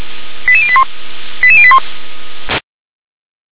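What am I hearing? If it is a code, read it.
Five-tone selective call (ZVEI), heard as quick runs of five tones, tones of about 70 ms. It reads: 88081 88081
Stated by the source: SDR-received ZVEI five-tone selective call, code 88149.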